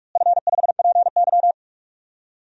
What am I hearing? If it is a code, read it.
Morse code, heard as a keyed tone, about 46 wpm, 700 Hz. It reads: F5PY
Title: DXCC-100 Call Signs - 4 Characters - Repeated 2x - 45wpm